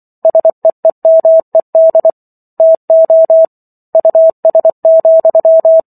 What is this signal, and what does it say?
Morse code, heard as a keyed tone, about 24 wpm, 650 Hz. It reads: SEEMED TO US,